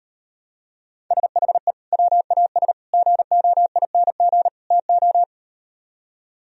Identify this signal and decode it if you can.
Morse code, heard as a keyed tone, about 38 wpm, 700 Hz. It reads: SHE WAS GOING TO